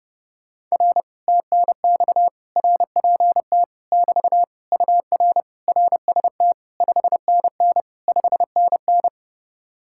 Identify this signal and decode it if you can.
Morse code, heard as a keyed tone, about 31 wpm, 700 Hz. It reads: R TNX RPT <BT> UR RST 5NN 5NN